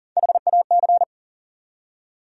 Morse code, heard as a keyed tone, about 40 wpm, 700 Hz. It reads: HAC
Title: USA Call Sign Postfixes - 3 Letters 40wpm